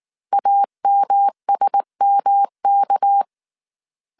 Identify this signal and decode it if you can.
Morse code, heard as a keyed tone, about 19 wpm, 800 Hz. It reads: AMSMK